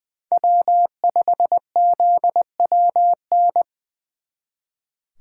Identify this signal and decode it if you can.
Morse code, heard as a keyed tone, about 23 wpm, 700 Hz. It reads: W5ZWN
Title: Speed Racing - USA 1x3 Call Signs 20wpm